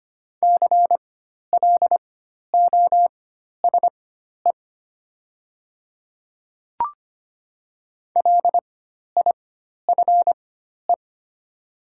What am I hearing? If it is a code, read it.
Morse code, heard as a keyed tone, about 25 wpm, spaced as for 12 wpm, 700 Hz. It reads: CLOSE LIFE
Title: 4x Character Spacing - Top 300 Words - 25wpm